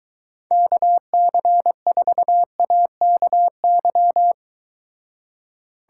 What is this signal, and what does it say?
Morse code, heard as a keyed tone, about 22 wpm, 700 Hz. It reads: KC4AKY